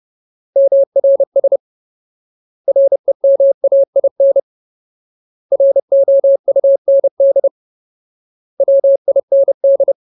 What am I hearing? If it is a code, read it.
Morse code, heard as a keyed tone, about 30 wpm, 550 Hz. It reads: MRS REMAIN ROUND WIND
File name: Ultimate Word List / 30WPM Ultimate